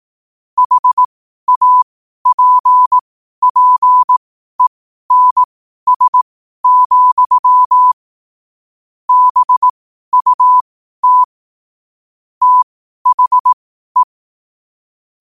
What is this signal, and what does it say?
Morse code, heard as a keyed tone, about 18 wpm, spaced as for 12 wpm, 1000 Hz. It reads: HAPPENS, BUT THE
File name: qotdfiles / 12wpm